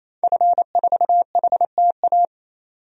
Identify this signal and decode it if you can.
Morse code, heard as a keyed tone, about 28 wpm, 700 Hz. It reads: F4HTA
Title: DXCC-100 Call Signs - 5 Characters - 28wpm